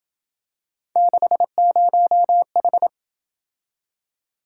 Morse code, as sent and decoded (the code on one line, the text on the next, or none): -.... ----- ....
60H